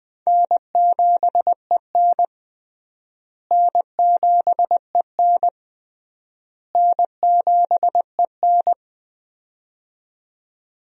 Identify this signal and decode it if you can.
Morse code, heard as a keyed tone, about 20 wpm, 700 Hz. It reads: N7EN N7EN N7EN
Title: USA Call Signs - 4 Characters - Repeated 3x - 20wpm